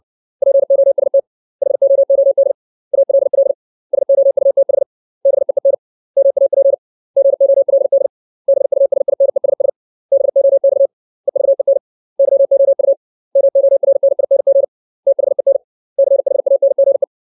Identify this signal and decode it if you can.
Morse code, heard as a keyed tone, about 60 wpm, 550 Hz. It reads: POST HOOD ALL SOUTH BEEN MAP GOLD BUSINESS BOX EVEN YOU MORNING THEN CHANGE
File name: 632 words / 632 Words 60WPM